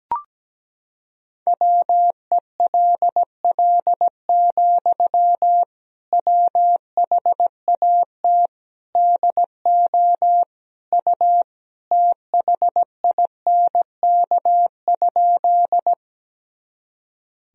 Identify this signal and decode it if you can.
Morse code, heard as a keyed tone, about 17 wpm, 700 Hz. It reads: WELL, WHAT DO U THINK?
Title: Sentences from Top 200 Words and Abbreviations 17wpm